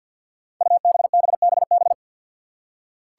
Morse code, heard as a keyed tone, about 45 wpm, 700 Hz. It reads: UBBBB
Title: Five Challenging Letter-Number Combinations 50wpm